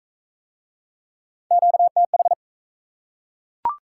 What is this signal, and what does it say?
Morse code, heard as a keyed tone, about 40 wpm, 700 Hz. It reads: QTH